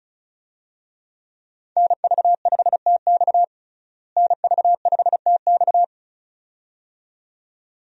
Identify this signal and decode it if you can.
Morse code, heard as a keyed tone, about 36 wpm, 700 Hz. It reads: NV5TX NV5TX
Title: USA Call Signs - 5 Characters - Repeated 2x - 35wpm